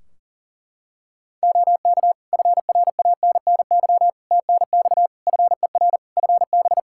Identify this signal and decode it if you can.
Morse code, heard as a keyed tone, about 40 wpm, 700 Hz. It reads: OK FRANNY TNX FER FB
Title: Practice QSO - Exchange 4 - Ending - 40wpm